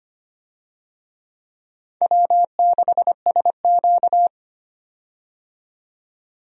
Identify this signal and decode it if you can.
Morse code, heard as a keyed tone, about 25 wpm, 700 Hz. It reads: W6SQ
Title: USA Call Signs - 4 Characters - 25wpm